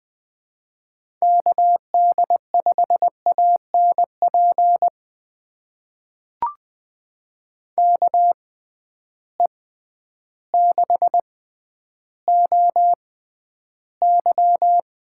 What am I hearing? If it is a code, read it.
Morse code, heard as a keyed tone, about 20 wpm, spaced as for 11 wpm, 700 Hz. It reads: KD5ANP KE6OY
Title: USA Call Signs - 6 Characters - 6x Character Spacing - 20wpm